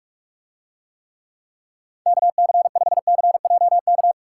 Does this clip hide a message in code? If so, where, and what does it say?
Yes: Morse code from 2.1 to 4.1 s, heard as a keyed tone, about 44 wpm, 700 Hz. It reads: KC5CJK